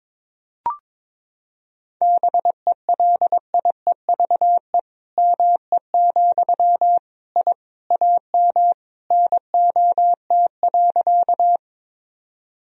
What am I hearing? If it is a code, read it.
Morse code, heard as a keyed tone, about 22 wpm, 700 Hz. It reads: BELIEVE ME, I AM NOT.
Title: Sentences from Top 1000 Words - Review Part 2 - 22wpm